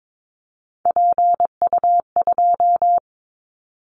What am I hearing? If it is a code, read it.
Morse code, heard as a keyed tone, about 22 wpm, 700 Hz. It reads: PU2
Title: Callsign Prefixes for Top 50 DXCC - 22wpm